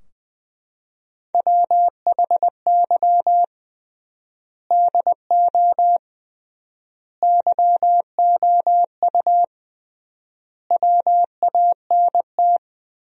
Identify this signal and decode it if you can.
Morse code, heard as a keyed tone, about 20 wpm, 700 Hz. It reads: WHY DO YOU WANT
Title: 3x Word-Spacing - Sentences from Top 200 Words - 20wpm